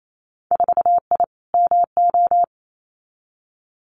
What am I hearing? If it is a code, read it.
Morse code, heard as a keyed tone, about 28 wpm, 700 Hz. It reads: 4I MO
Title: Winter Field Day Exchanges 28wpm